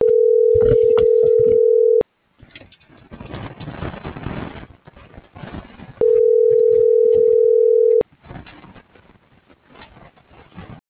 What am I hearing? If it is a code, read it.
Ringback tone (North America), 440 + 480 Hz.
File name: CENTRAL GENERAL / out america.mejia